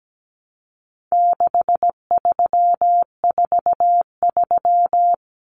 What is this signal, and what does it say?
Morse code, heard as a keyed tone, about 17 wpm, 700 Hz. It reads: 6343
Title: Common Cut Numbers - 4-Digit 17wpm